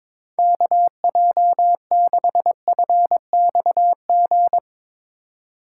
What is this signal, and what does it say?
Morse code, heard as a keyed tone, about 24 wpm, 700 Hz. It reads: KJ6FXG
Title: Speed Racing - USA 2x3 Call Signs 22wpm